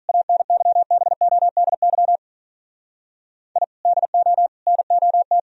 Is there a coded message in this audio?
Yes: Morse code from 0.1 to 5.4 s, heard as a keyed tone, about 46 wpm, 700 Hz. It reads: ANYBODY I DO NOT